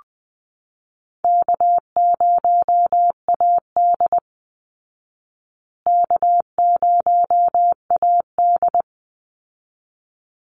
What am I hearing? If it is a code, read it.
Morse code, heard as a keyed tone, about 20 wpm, 700 Hz. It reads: K0AD K0AD
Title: Call Signs Repeated 3x - 20wpm